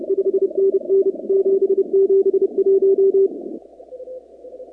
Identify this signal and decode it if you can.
Morse code, heard as a keyed tone, about 31 wpm, 390 Hz. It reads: 5NN 771
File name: contests / 2019-cw-wpx-kt5j-k5pi- -2330 0-44